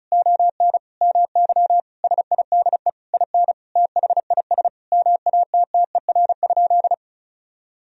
Morse code, heard as a keyed tone, about 36 wpm, 700 Hz. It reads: ON MY SIDE IN THIS MATTER?
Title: Sentences from Top 1000 Words - Review Part 1 - 35wpm